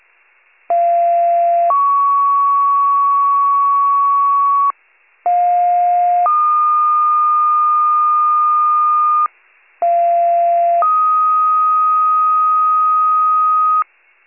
Two-tone page, heard as a one-second tone, then a three-second tone, about 690 then 1120 Hz, about 710 then 1220 Hz, about 690 then 1250 Hz. It is heard from 0.7 to 13.8 s.